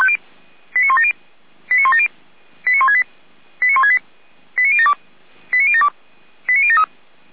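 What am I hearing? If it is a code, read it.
Five-tone selective call (ZVEI), heard as quick runs of five tones, tones of about 70 ms. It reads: 89170 89170 89168 89168 89072 89072 89073
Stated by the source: ZVEI five-tone selective call 89560, received on an SDR.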